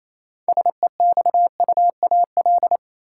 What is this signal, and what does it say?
Morse code, heard as a keyed tone, about 28 wpm, 700 Hz. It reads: SEXUAL
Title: Six Letter Words - 28wpm